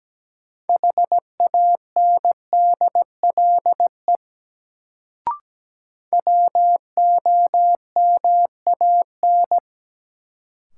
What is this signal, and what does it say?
Morse code, heard as a keyed tone, about 17 wpm, 700 Hz. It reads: HANDLE WOMAN
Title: Sentences from Top 1100 Words - New Word Drill Alt - 17wpm